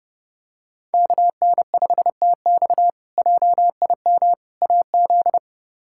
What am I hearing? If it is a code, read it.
Morse code, heard as a keyed tone, about 30 wpm, 700 Hz. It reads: KN5TX JIM AZ
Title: K1USN SST QSO Practice - 30wpm